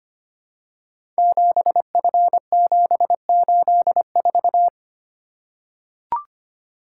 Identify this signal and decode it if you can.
Morse code, heard as a keyed tone, about 25 wpm, 700 Hz. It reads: 7F784
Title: Code Groups - 5 Letter:Numbers - 25wpm